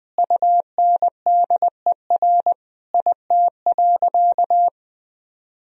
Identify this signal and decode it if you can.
Morse code, heard as a keyed tone, about 20 wpm, 700 Hz. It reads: UNDER IT.
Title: Sentences from Top 1000 Words - Review Part 1 - 20wpm